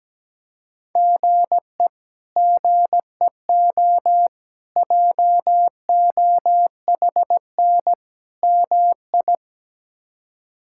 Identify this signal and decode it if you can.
Morse code, heard as a keyed tone, about 17 wpm, 700 Hz. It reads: GE GEO JOHN MI
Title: K1USN SST QSO Practice - 17wpm